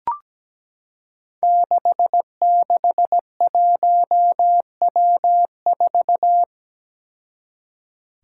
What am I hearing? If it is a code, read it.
Morse code, heard as a keyed tone, about 17 wpm, 700 Hz. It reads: 661W4
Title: Five Challenging Letter-Number Combinations 17wpm